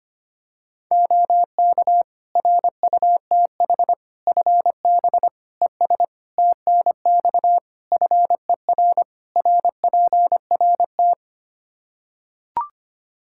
Morse code, heard as a keyed tone, about 25 wpm, 700 Hz. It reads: OK RUTH FB ES TNX FER RPRT
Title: Practice QSO - Exchange 2 - Info - 25wpm